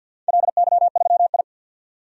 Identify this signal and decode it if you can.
Morse code, heard as a keyed tone, about 47 wpm, 700 Hz. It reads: LY3I
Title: DXCC-100 Call Signs - 4 Characters - 50wpm